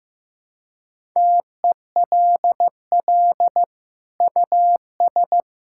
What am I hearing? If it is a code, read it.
Morse code, heard as a keyed tone, about 15 wpm, 700 Hz. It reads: TELL US
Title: Sentences from Top 1000 Words - Review Part 6 - 15wpm